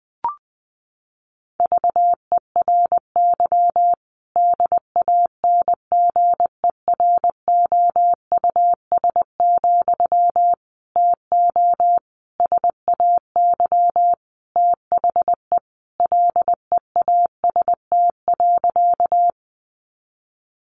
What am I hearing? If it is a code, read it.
Morse code, heard as a keyed tone, about 20 wpm, 700 Hz. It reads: VERY DANGEROUS, TO SAY THE LEAST.
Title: Sentences from Top 1200 Words 20wpm